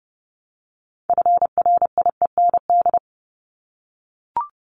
Morse code, heard as a keyed tone, about 30 wpm, 700 Hz. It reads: FRIEND
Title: Top 500 Words 30wpm